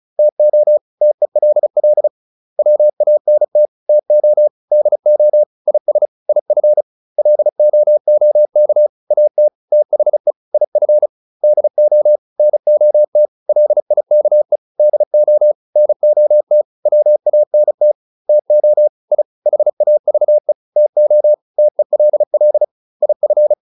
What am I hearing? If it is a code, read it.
Morse code, heard as a keyed tone, about 34 wpm, 600 Hz. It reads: TO TELL WANT TO DO IS IF LOOK AT THE IF DO NOT LIKE DO NOT WANT TO I HAVE TO TELL IF